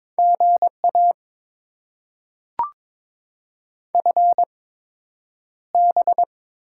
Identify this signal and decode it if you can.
Morse code, heard as a keyed tone, about 22 wpm, spaced as for 9 wpm, 700 Hz. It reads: GA FB
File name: Top 20 QSO Elements - 8x Character Spacing - 22wpm